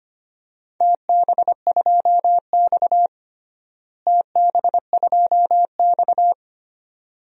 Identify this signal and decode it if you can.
Morse code, heard as a keyed tone, about 25 wpm, 700 Hz. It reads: TB2X TB2X
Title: DXCC-100 Call Signs - 4 Characters - Repeated 2x - 25wpm